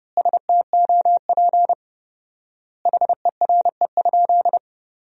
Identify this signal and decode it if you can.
Morse code, heard as a keyed tone, about 30 wpm, 700 Hz. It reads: STOP HERE?
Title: 4x Word-Spacing - Sentences from Top 300 Words - 30wpm